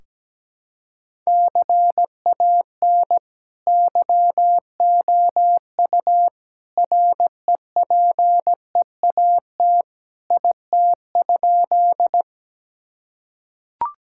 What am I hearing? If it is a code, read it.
Morse code, heard as a keyed tone, about 17 wpm, 700 Hz. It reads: CAN YOU REPEAT IT?
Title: Sentences from Top 1600 Words 17wpm